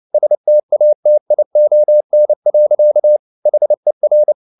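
Morse code, heard as a keyed tone, about 29 wpm, 600 Hz. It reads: STATION. HER
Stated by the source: Morse code at 29 wpm, sending Ken Follett's novel Never in English, from CW med Gnister.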